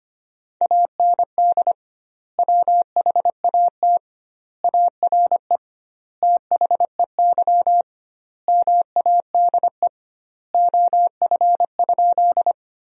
Morse code, heard as a keyed tone, about 25 wpm, 700 Hz. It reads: AND WHAT ARE THEY MADE OF?
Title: 2x Word-Spacing - Sentences from Top 200 Words - 25wpm